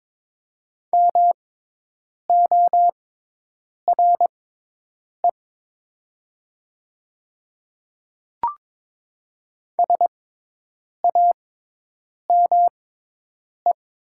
Morse code, heard as a keyed tone, about 22 wpm, spaced as for 8 wpm, 700 Hz. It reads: MORE SAME
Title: 6x Character Spacing - Top 200 Words - 22wpm